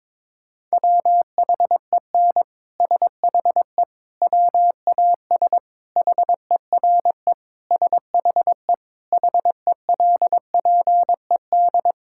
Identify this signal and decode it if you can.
Morse code, heard as a keyed tone, about 22 wpm, 700 Hz. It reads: WHEN SHE WAS HERE SHE HELPED
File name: practice / 22wpm / Sentences with 3-Word Phrases - 22wpm